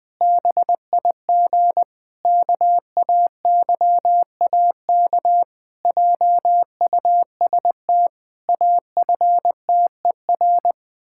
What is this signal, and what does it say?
Morse code, heard as a keyed tone, about 20 wpm, 700 Hz. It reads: BIG KAYAK JUST AFTER
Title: Pangram Challenge v2 - 20wpm